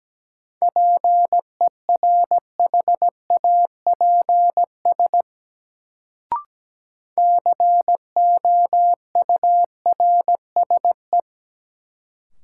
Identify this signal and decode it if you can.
Morse code, heard as a keyed tone, about 17 wpm, 700 Hz. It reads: PERHAPS COURSE
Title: Sentences from Top 500 Words - New Word Drill - 17wpm